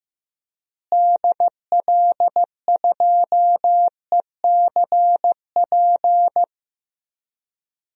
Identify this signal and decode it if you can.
Morse code, heard as a keyed tone, about 15 wpm, 700 Hz. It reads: DL2ECP
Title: DXCC-100 Call Signs - 6 Characters - Repeated 3x - 15wpm